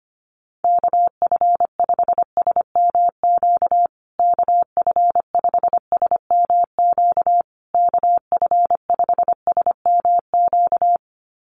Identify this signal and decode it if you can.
Morse code, heard as a keyed tone, about 25 wpm, 700 Hz. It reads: KF5SMQ KF5SMQ KF5SMQ
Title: USA Call Signs Repeated 3x 25wpm